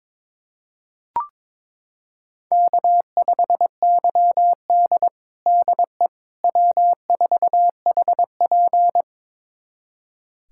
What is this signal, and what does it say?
Morse code, heard as a keyed tone, about 22 wpm, 700 Hz. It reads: K5YD DE W4HP
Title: USA Callsign DE Callsign - 4 Characters - 22wpm